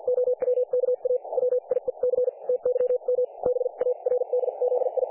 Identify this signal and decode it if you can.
Morse code, heard as a keyed tone, about 50 wpm, spaced as for 42 wpm, 500 Hz. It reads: OWKA WIEC TYM BARDZI
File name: dlazaawansowanych / nagraniazpasma / websdr recording 3564.7kHz